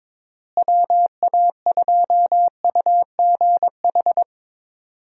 Morse code, heard as a keyed tone, about 22 wpm, 700 Hz. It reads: WA2UGH
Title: USA Call Signs - 6 Characters - Repeated 2x - 22wpm